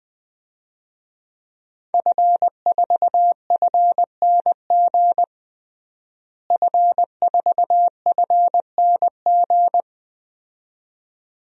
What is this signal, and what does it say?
Morse code, heard as a keyed tone, about 20 wpm, 700 Hz. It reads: F4FNG F4FNG
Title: DXCC-100 Call Signs - 5 Characters - Repeated 2x - 20wpm